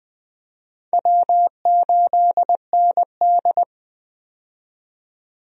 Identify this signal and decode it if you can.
Morse code, heard as a keyed tone, about 20 wpm, 700 Hz. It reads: W8ND
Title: USA Call Signs - 4 Characters - 20wpm